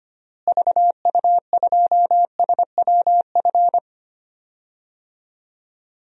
Morse code, heard as a keyed tone, about 25 wpm, 700 Hz. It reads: VU2SWF
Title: DXCC-100 Call Signs - 6 Characters - Repeated 2x - 25wpm